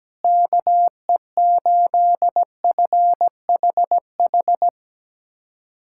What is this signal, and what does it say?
Morse code, heard as a keyed tone, about 17 wpm, 700 Hz. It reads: KE8FHH
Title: USA Callsign DE Callsign 17wpm